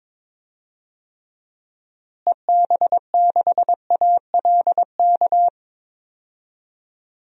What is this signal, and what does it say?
Morse code, heard as a keyed tone, about 21 wpm, 700 Hz. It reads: EB6ALK
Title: DXCC-100 Call Signs - 6 Characters - Repeated 3x - 22wpm